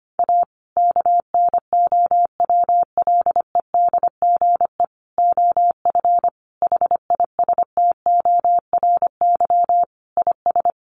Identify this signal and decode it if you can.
Morse code, heard as a keyed tone, about 25 wpm, 700 Hz. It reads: A KNOWLEDGE OF HISTORY IS
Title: Sentences from Top 900 Words 25wpm